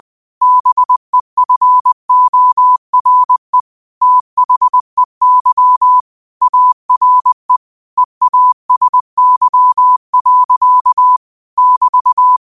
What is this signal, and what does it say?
Morse code, heard as a keyed tone, about 20 wpm, 1000 Hz. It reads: BEFORE THEY ARE EASY. <BT>
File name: qotdfiles / 20wpm